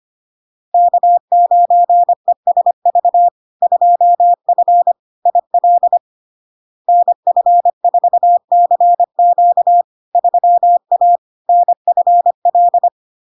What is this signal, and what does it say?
Morse code, heard as a keyed tone, about 25 wpm, 700 Hz. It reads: K9ESV 2F IL NF4CQ 3A NFL
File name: Log-in-CW-25wpm-700-Hz